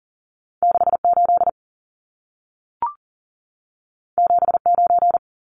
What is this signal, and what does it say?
Morse code, heard as a keyed tone, about 40 wpm, 700 Hz. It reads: 68 79
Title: Numbers - 2-Digit 40wpm